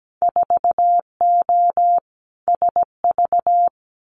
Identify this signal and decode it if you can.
Morse code, heard as a keyed tone, about 17 wpm, 700 Hz. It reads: 4O SV
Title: Winter Field Day Exchanges Encore 17wpm